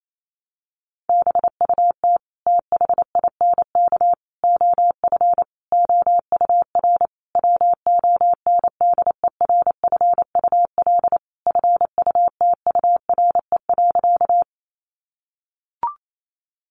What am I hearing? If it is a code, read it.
Morse code, heard as a keyed tone, about 28 wpm, 700 Hz. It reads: BUT THINK OF OUR WONDERFUL FUTURE.
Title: Sentences from Top 1100 Words 28wpm